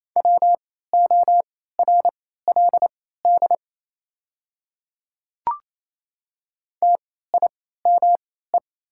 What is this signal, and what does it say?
Morse code, heard as a keyed tone, about 28 wpm, spaced as for 16 wpm, 700 Hz. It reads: WORLD TIME
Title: 3x Character Spacing - Top 200 Words - 28wpm